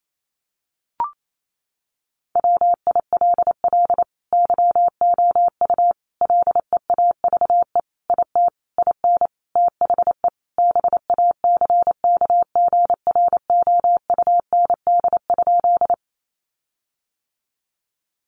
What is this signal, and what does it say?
Morse code, heard as a keyed tone, about 28 wpm, 700 Hz. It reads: WILL YOU LEAVE IT IN THE BACKGROUND?
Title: Sentences from Top 1300 Words 28wpm